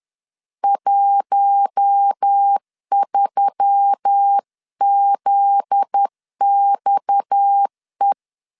Morse code, heard as a keyed tone, about 10 wpm, 800 Hz. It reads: 13ZXE
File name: LEZIONI / 7 GF9Z / 7 GF9Z 10wpm 2611